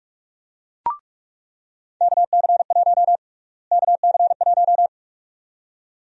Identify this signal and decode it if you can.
Morse code, heard as a keyed tone, about 46 wpm, 700 Hz. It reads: KC1 KC1
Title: USA Call Sign Prefixes - Repeated 2x - 45wpm